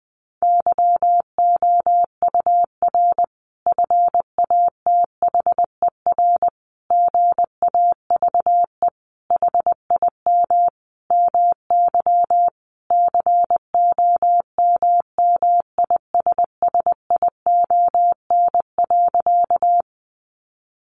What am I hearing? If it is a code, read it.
Morse code, heard as a keyed tone, about 20 wpm, 700 Hz. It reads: YOUR FATHER GAVE HIM MY COMMISSION.
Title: Sentences from Top 1400 Words 20wpm